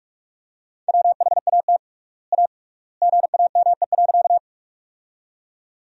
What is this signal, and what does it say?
Morse code, heard as a keyed tone, about 46 wpm, 700 Hz. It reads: WHAT A GAME.
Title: 3x Word-Spacing - Sentences from Top 500 Words - 45wpm